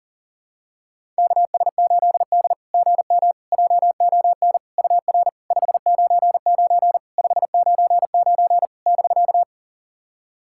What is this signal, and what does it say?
Morse code, heard as a keyed tone, about 40 wpm, 700 Hz. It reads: KS8D GM JON UR 599 599 <BK>
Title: SOTA QSO Practice 40wpm